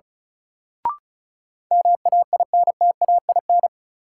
Morse code, heard as a keyed tone, about 36 wpm, 700 Hz. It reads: MAINTAIN
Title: Sentences from Top 1000 Words - Review Part 5 - New Word Drill - 35wpm